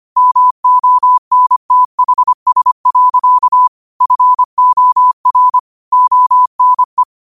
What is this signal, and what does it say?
Morse code, heard as a keyed tone, about 25 wpm, 1000 Hz. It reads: MONTHS. FOR ONE
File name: qotdfiles / 25wpm